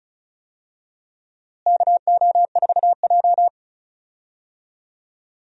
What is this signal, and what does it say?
Morse code, heard as a keyed tone, about 36 wpm, 700 Hz. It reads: KO4J
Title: USA Call Signs - 4 Characters - Repeated 2x - 35wpm